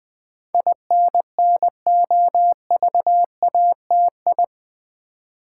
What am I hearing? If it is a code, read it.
Morse code, heard as a keyed tone, about 20 wpm, 700 Hz. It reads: INNOVATI
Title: Build A Word - 4 Letters At A Time - 10-Letter Words - 20wpm